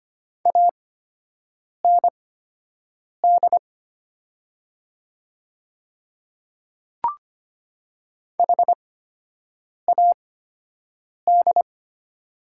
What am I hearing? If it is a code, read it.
Morse code, heard as a keyed tone, about 25 wpm, spaced as for 7 wpm, 700 Hz. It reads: AND HAD
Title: 8x Character Spacing - Top 100 Words - 25wpm